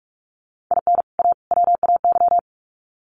Morse code, heard as a keyed tone, about 44 wpm, 700 Hz. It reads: IN A WAY
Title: Sets of 3 Words 45wpm